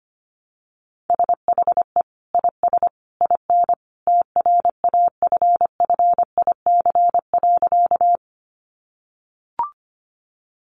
Morse code, heard as a keyed tone, about 25 wpm, 700 Hz. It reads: SHE IS IN TRAFFIC.
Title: Sentences from Top 1300 Words 25wpm